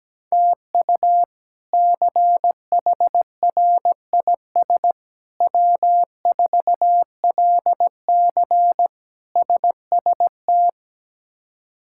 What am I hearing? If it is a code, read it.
Morse code, heard as a keyed tone, about 17 wpm, 700 Hz. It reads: TU CHRIS W4LC SST
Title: K1USN SST QSO Practice - 17wpm